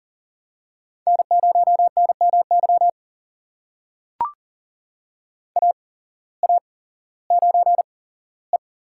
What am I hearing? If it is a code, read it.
Morse code, heard as a keyed tone, about 43 wpm, spaced as for 19 wpm, 700 Hz. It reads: N0NMY AA9E
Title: USA Call Signs - 5 Characters - 8x Character Spacing - 40wpm